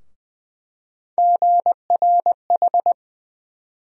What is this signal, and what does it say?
Morse code, heard as a keyed tone, about 21 wpm, 700 Hz. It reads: GRH
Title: USA Call Sign Postfixes - 3 Letters 20wpm